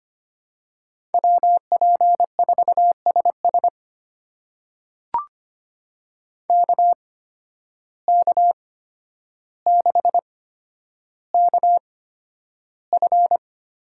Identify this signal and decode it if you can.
Morse code, heard as a keyed tone, about 25 wpm, spaced as for 11 wpm, 700 Hz. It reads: WP4SS KK6KF
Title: USA Call Signs - 5 Characters - 8x Character Spacing - 25wpm